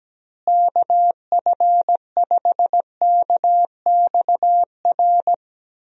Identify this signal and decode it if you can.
Morse code, heard as a keyed tone, about 17 wpm, 700 Hz. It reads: KF5KXR
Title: USA Call Signs - 6 Characters - Repeated 3x - 17wpm